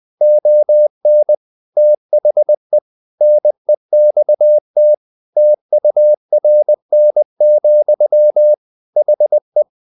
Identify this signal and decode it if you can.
Morse code, heard as a keyed tone, about 20 wpm, 600 Hz. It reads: ON THE NEXT TURN, HE
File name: the old man and the sea part 2 20wpm